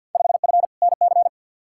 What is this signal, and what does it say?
Morse code, heard as a keyed tone, about 53 wpm, 700 Hz. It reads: 5F NC
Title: Field Day Exchanges Encore - 50wpm